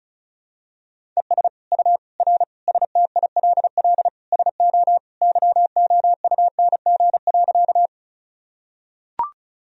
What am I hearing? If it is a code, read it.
Morse code, heard as a keyed tone, about 34 wpm, 700 Hz. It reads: ES U R STILL SO YOUNG.
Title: Sentences from Top 400 Words and Abbreviations 35wpm